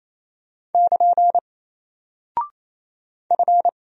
Morse code, heard as a keyed tone, about 27 wpm, 700 Hz. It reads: <KN> F
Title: Single Character (Letter:Number:Punctuation:Prosign) 28wpm